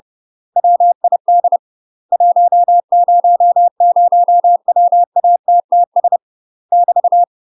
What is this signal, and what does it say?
Morse code, heard as a keyed tone, about 30 wpm, 700 Hz. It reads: WID 100WATTS <BT>
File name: UK MASTER QSOs TRAINING / 30WPM / UK MASTER QSOs 0029